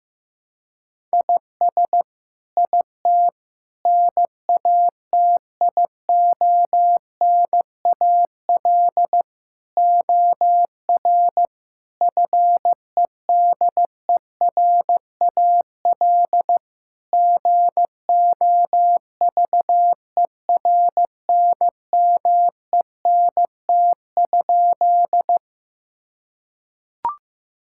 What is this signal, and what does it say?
Morse code, heard as a keyed tone, about 15 wpm, 700 Hz. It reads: IS IT NATIONAL OR FEDERAL GOVERNMENT?
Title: Sentences from Top 1000 Words - Review Part 5 - 15wpm